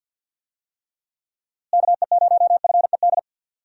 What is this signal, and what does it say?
Morse code, heard as a keyed tone, about 52 wpm, 700 Hz. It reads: KE0FED
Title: USA Call Signs - 6 Characters - Repeated 3x - 50wpm